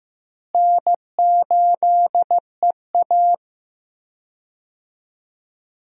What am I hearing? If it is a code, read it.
Morse code, heard as a keyed tone, about 15 wpm, 700 Hz. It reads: N8EA
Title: USA Call Signs - 4 Characters - Repeated 3x - 15wpm